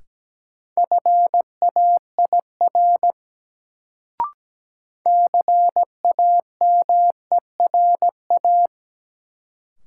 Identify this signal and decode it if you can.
Morse code, heard as a keyed tone, about 17 wpm, 700 Hz. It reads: FAIR CAMERA
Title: Sentences from Top 1000 Words - Review Part 5 - New Word Drill - 17wpm